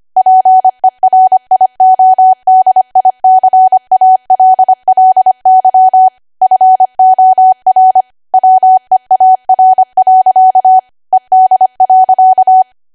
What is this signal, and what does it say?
Morse code, heard as a keyed tone, about 25 wpm, 750 Hz. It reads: PERIODICALLY FOR WEAR. ED.
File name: Archive / 25 WPM / 25WPM